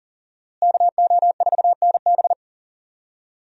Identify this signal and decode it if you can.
Morse code, heard as a keyed tone, about 40 wpm, 700 Hz. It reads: KO4NB
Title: USA Call Signs - 5 Characters - Repeated 3x - 40wpm